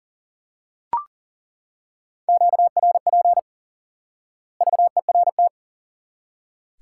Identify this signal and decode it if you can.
Morse code, heard as a keyed tone, about 40 wpm, 700 Hz. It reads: QRP VERT
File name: IQE Course 3 - Lesson 89 - YAGI - Two Elements - 40wpm